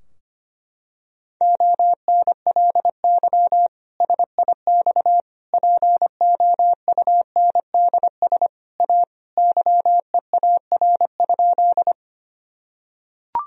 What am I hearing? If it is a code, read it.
Morse code, heard as a keyed tone, about 25 wpm, 700 Hz. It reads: ONLY SIX POUNDS A YEAR?
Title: Sentences from Top 1000 Words - Review Part 2 - 25wpm